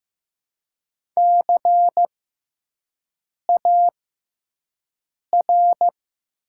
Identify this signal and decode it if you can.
Morse code, heard as a keyed tone, about 15 wpm, spaced as for 5 wpm, 700 Hz. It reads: CAR
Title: 6x Character Spacing - Top 500 Words - 15wpm